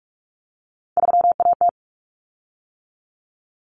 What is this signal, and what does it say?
Morse code, heard as a keyed tone, about 44 wpm, 700 Hz. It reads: 3AT
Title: Common Cut Numbers - 3-Digit 45wpm